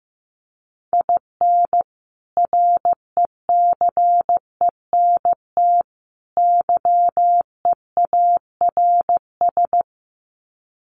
Morse code, heard as a keyed tone, about 15 wpm, 700 Hz. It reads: IN RECENT YEARS
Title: Sets of 3 Words 15wpm